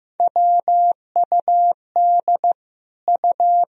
Morse code, heard as a keyed tone, about 15 wpm, 700 Hz. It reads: WUD U
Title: Sentences from Top 100 Words and Abbreviations 15wpm